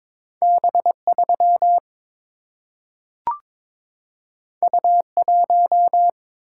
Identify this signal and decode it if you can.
Morse code, heard as a keyed tone, about 22 wpm, 700 Hz. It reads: B3 U1
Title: Single-Letter Single-Number 22wpm